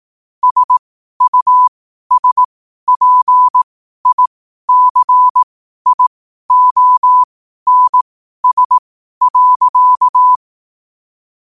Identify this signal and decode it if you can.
Morse code, heard as a keyed tone, about 18 wpm, spaced as for 12 wpm, 1000 Hz. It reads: SUSPICIONS.